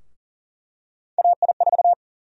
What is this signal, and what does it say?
Morse code, heard as a keyed tone, about 40 wpm, 700 Hz. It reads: AI4
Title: USA Call Sign Prefixes - 40wpm